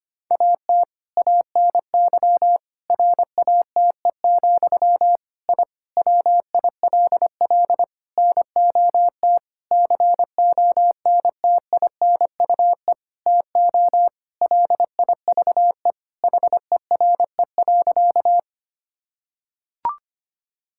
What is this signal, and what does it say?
Morse code, heard as a keyed tone, about 25 wpm, 700 Hz. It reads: AT ANY RATE, I WILL NOT CONTINUE TO LIVE HERE.
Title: Sentences from Top 1000 Words - Review Part 6 - 25wpm